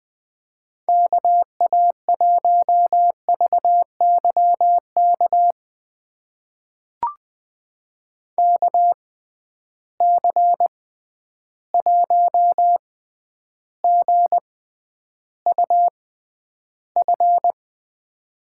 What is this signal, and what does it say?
Morse code, heard as a keyed tone, about 20 wpm, spaced as for 10 wpm, 700 Hz. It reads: KA1VYK KC1GUF
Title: USA Call Signs - 6 Characters - 6x Character Spacing - 20wpm